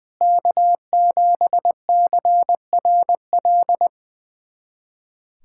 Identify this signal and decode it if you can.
Morse code, heard as a keyed tone, about 20 wpm, 700 Hz. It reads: K7CRL
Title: USA Callsign DE Callsign - 5 Characters - 20wpm